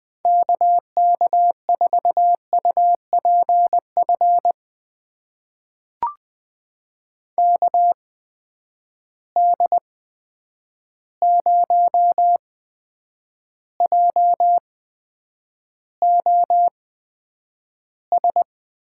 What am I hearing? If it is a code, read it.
Morse code, heard as a keyed tone, about 20 wpm, spaced as for 9 wpm, 700 Hz. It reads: KK4UPF KD0JOS
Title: USA Call Signs - 6 Characters - 8x Character Spacing - 20wpm